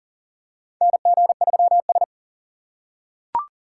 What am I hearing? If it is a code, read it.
Morse code, heard as a keyed tone, about 40 wpm, 700 Hz. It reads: NG3S
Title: USA Call Signs - 4 Characters - 6x Character Spacing - 40wpm